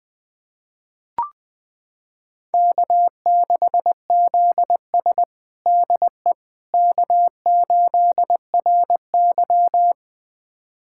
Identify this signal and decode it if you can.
Morse code, heard as a keyed tone, about 21 wpm, 700 Hz. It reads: K6ZS DE K8RY
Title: USA Callsign DE Callsign - 4 Characters - 20wpm